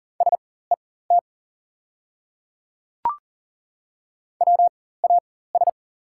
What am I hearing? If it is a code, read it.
Morse code, heard as a keyed tone, about 39 wpm, spaced as for 19 wpm, 690 Hz. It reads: SET WAS
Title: 4x Character Spacing - Top 200 Words - 40wpm